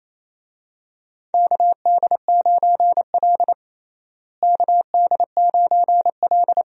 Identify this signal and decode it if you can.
Morse code, heard as a keyed tone, about 28 wpm, 700 Hz. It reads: KD9L KD9L
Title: USA Call Signs - 4 Characters - Repeated 3x - 28wpm